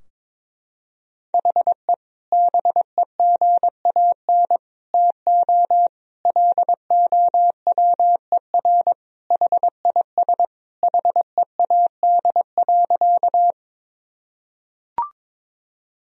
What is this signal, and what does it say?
Morse code, heard as a keyed tone, about 22 wpm, 700 Hz. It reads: HE BEGAN TO LOWER HIS HEAD.
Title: Sentences from Top 1000 Words - Review Part 4 - 22wpm